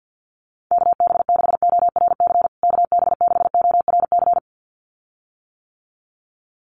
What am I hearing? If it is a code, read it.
Morse code, heard as a keyed tone, about 50 wpm, 700 Hz. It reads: KB6ORC KB6ORC
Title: USA Call Signs Repeated 2x 50wpm